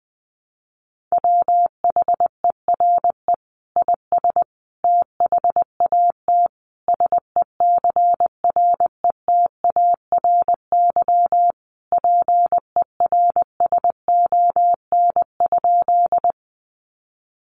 Morse code, heard as a keyed tone, about 20 wpm, 700 Hz. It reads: WHERE IS THAT SECRETARY PERSON?
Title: Sentences from Top 1300 Words 20wpm